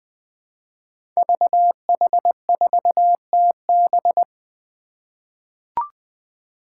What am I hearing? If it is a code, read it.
Morse code, heard as a keyed tone, about 20 wpm, 700 Hz. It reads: VH4TB
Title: Five Challenging Letter-Number Combinations 20wpm